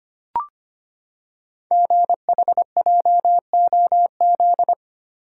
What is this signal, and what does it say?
Morse code, heard as a keyed tone, about 25 wpm, 700 Hz. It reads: GHJOZ
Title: Code Groups - 5 Letters - 25wpm